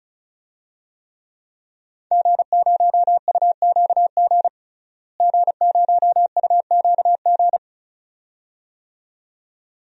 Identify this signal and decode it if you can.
Morse code, heard as a keyed tone, about 34 wpm, 700 Hz. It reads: G0UQG G0UQG
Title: DXCC-100 Call Signs - 5 Characters - Repeated 2x - 35wpm